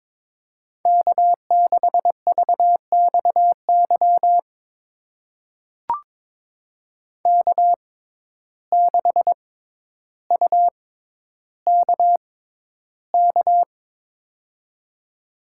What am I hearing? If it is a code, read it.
Morse code, heard as a keyed tone, about 22 wpm, spaced as for 11 wpm, 700 Hz. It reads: K6VXY K6UKK
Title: USA Call Signs - 5 Characters - 6x Character Spacing - 22wpm